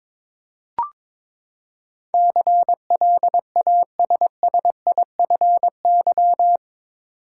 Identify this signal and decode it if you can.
Morse code, heard as a keyed tone, about 21 wpm, 700 Hz. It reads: CLASSIFY
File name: practice / 22wpm / Eight Letter Words - 22wpm